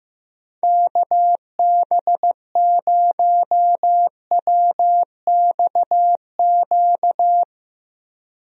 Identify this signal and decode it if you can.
Morse code, heard as a keyed tone, about 15 wpm, 700 Hz. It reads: KB0WXQ
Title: USA Call Signs - 6 Characters - Repeated 2x - 15wpm